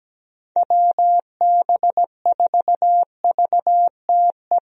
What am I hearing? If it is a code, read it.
Morse code, heard as a keyed tone, about 17 wpm, 700 Hz. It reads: WB4VTE
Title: USA Call Signs - 6 Characters - Repeated 3x - 17wpm